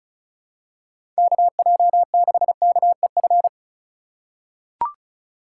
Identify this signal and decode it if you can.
Morse code, heard as a keyed tone, about 36 wpm, 700 Hz. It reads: KJ6KEF
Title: USA Call Signs - 6 Characters - 2x Character Spacing - 35wpm